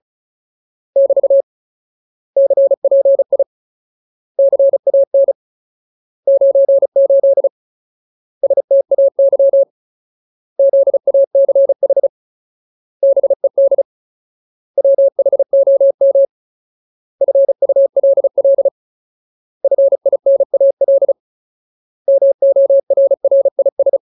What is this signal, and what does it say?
Morse code, heard as a keyed tone, about 34 wpm, 550 Hz. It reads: <BT> CPI CAN 98 STAY ZACH BED WHOM FULL FINAL MORRIS
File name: Ultimate Word List / 35WPM Ultimate